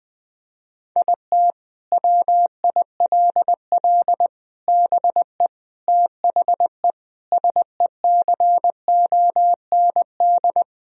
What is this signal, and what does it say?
Morse code, heard as a keyed tone, about 20 wpm, 700 Hz. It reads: IT WILL BE THE SECOND
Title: Sentences from Top 1000 Words - Review Part 2 - 20wpm